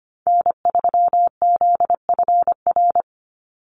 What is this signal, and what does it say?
Morse code, heard as a keyed tone, about 25 wpm, 700 Hz. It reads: N3ZFR
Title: USA Call Signs 25wpm